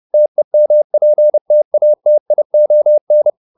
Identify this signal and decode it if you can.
Morse code, heard as a keyed tone, about 30 wpm, 600 Hz. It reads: TEMPTATION